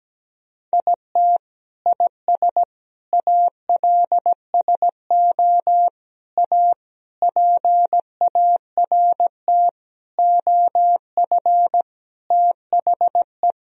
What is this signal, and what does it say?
Morse code, heard as a keyed tone, about 17 wpm, 700 Hz. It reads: IT IS ALSO A PART OF THE
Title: Sentences from Top 1000 Words - Review Part 1 - 17wpm